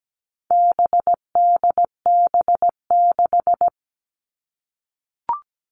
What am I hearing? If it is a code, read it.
Morse code, heard as a keyed tone, about 17 wpm, 700 Hz. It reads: BDB6
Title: All Cut Numbers - 4-Digit 17wpm